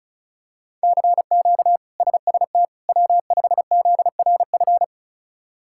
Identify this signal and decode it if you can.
Morse code, heard as a keyed tone, about 36 wpm, 700 Hz. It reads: CQ SST W5ZRF
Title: K1USN SST QSO Practice - 35wpm